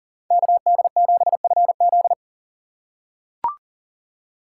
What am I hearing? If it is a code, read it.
Morse code, heard as a keyed tone, about 42 wpm, 690 Hz. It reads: KD7FZ
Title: USA Call Signs - 5 Characters - 3x Character Spacing - 40wpm